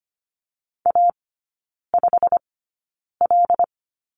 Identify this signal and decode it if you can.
Morse code, heard as a keyed tone, about 25 wpm, spaced as for 9 wpm, 700 Hz. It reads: A5L
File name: course / 25:10wpm / 093 Lesson 16e - Letter c - Three Characters 25wpm